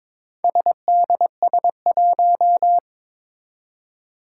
Morse code, heard as a keyed tone, about 22 wpm, 700 Hz. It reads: SDS1
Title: Four Challenging Letter-Number Combinations 22wpm